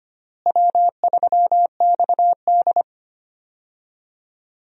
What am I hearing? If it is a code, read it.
Morse code, heard as a keyed tone, about 25 wpm, 700 Hz. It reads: W3XD
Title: USA Call Signs - 4 Characters - Repeated 3x - 25wpm